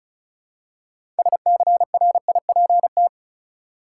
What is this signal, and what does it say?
Morse code, heard as a keyed tone, about 36 wpm, 700 Hz. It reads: SCRIPT
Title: Six Letter Words - 35wpm